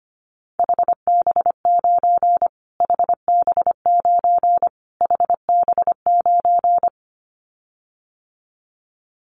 Morse code, heard as a keyed tone, about 25 wpm, 700 Hz. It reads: HB9 HB9 HB9
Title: Callsign Prefixes for Top 25 DXCC - Repeated 3x - 25wpm